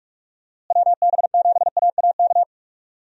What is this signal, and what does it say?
Morse code, heard as a keyed tone, about 41 wpm, 700 Hz. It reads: WB7AAK
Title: USA Call Signs - 6 Characters - Repeated 2x - 45wpm